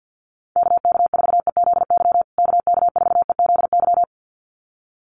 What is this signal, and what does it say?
Morse code, heard as a keyed tone, about 50 wpm, 700 Hz. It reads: KK4EZY KK4EZY
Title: USA Call Signs Repeated 2x 50wpm